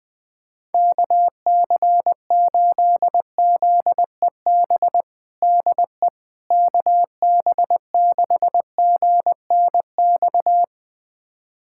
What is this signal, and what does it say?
Morse code, heard as a keyed tone, about 20 wpm, 700 Hz. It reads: KC8ZEB DE KB6GNX
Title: USA Callsign DE Callsign 20wpm